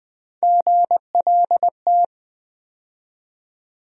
Morse code, heard as a keyed tone, about 20 wpm, 700 Hz. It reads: GLT
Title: USA Call Sign Postfixes - 3 Letters 20wpm